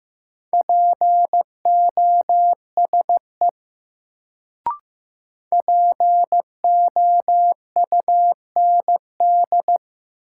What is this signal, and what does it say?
Morse code, heard as a keyed tone, about 15 wpm, 700 Hz. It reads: POSE POUND
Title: Sentences from Top 1000 Words - Review Part 2 - New Word Drill - 15wpm